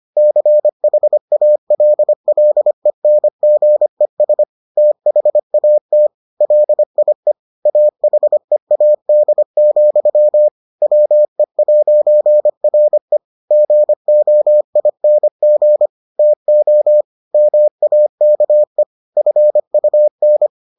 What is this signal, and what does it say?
Morse code, heard as a keyed tone, about 25 wpm, 600 Hz. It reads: CHALLENGES THAT LIE AHEAD, WE'RE GOING TO MAKE FUN